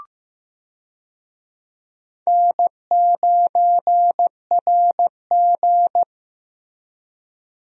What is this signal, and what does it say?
Morse code, heard as a keyed tone, about 15 wpm, 700 Hz. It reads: N9RG
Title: USA Call Signs - 4 Characters - Repeated 3x - 15wpm